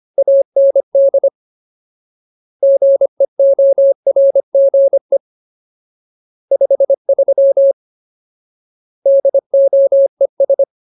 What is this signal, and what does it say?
Morse code, heard as a keyed tone, about 25 wpm, 550 Hz. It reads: AND GEORGE 53 DOES
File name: Ultimate Word List / 25WPM Ultimate